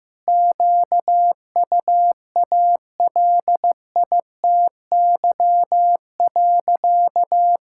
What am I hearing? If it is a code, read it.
Morse code, heard as a keyed tone, about 15 wpm, 700 Hz. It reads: QUALITY.
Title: Sentences from Top 1000 Words - Review Part 5 - 15wpm